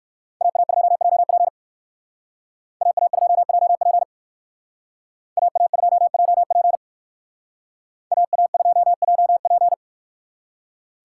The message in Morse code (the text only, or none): AA2JP AA2JP AA2JP AA2JP